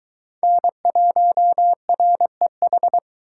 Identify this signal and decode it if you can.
Morse code, heard as a keyed tone, about 23 wpm, 700 Hz. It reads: N1REH